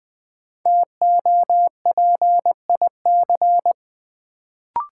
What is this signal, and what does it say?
Morse code, heard as a keyed tone, about 20 wpm, 700 Hz. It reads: TOPIC
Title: Sentences from Top 1200 Words - New Word Drill Alt - 20wpm